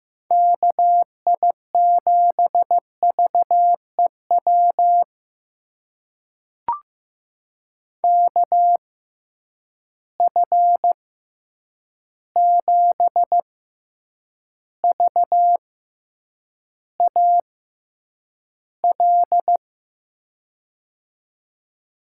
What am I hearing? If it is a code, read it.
Morse code, heard as a keyed tone, about 15 wpm, spaced as for 8 wpm, 700 Hz. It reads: KI7VEW KF7VAL
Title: USA Call Signs - 6 Characters - 6x Character Spacing - 15wpm